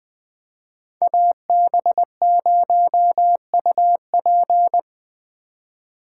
Morse code, heard as a keyed tone, about 20 wpm, 700 Hz. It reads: AB0UP
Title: Speed Racing - USA Call Signs 17wpm